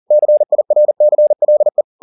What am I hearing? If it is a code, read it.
Morse code, heard as a keyed tone, about 40 wpm, 600 Hz. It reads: CIRCLE
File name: the old man and the sea 2 40wpm